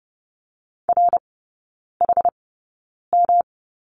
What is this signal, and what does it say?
Morse code, heard as a keyed tone, about 30 wpm, spaced as for 9 wpm, 700 Hz. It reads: RHM
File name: Lesson 18e - Letter m - Three Characters 30wpm